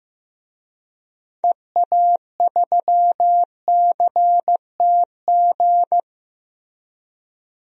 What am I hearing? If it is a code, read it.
Morse code, heard as a keyed tone, about 15 wpm, 700 Hz. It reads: EA3CTG
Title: DXCC-100 Call Signs - 6 Characters - Repeated 3x - 15wpm